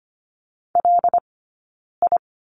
Morse code, heard as a keyed tone, about 25 wpm, spaced as for 9 wpm, 690 Hz. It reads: LI
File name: course / 25:10wpm / 092 Lesson 16d - Letter c - Two Characters 25wpm